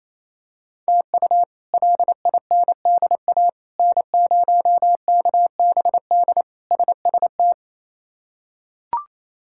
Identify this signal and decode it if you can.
Morse code, heard as a keyed tone, about 28 wpm, 700 Hz. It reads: TU LINDA N0KBD SST
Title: K1USN SST QSO Practice - 28wpm